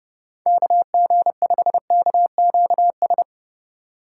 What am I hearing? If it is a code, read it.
Morse code, heard as a keyed tone, about 29 wpm, 700 Hz. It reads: KG5KQS
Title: USA Call Signs - 2x Character Spacing - 30wpm